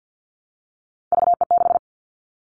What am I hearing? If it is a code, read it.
Morse code, heard as a keyed tone, about 49 wpm, 700 Hz. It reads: VE6
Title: Callsign Prefixes for Top 25 DXCC - Repeated 3x - 50wpm